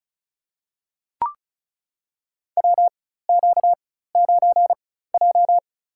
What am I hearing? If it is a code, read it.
Morse code, heard as a keyed tone, about 34 wpm, spaced as for 16 wpm, 700 Hz. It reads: WQ9J
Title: USA Call Signs - 4 Characters - 4x Character Spacing - 35wpm